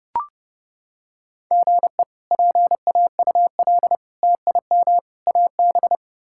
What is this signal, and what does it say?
Morse code, heard as a keyed tone, about 30 wpm, 700 Hz. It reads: GE PAUL TIM AB
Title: K1USN SST QSO Practice - 30wpm